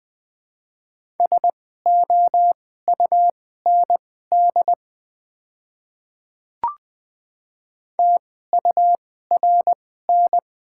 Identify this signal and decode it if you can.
Morse code, heard as a keyed tone, about 20 wpm, spaced as for 14 wpm, 700 Hz. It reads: SOUND TURN
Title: 2x Character Spacing - Top 200 Words - 20wpm